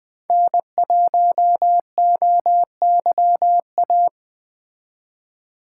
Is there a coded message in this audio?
Yes: Morse code from 0.3 to 4.1 s, heard as a keyed tone, about 21 wpm, 700 Hz. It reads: N1OYA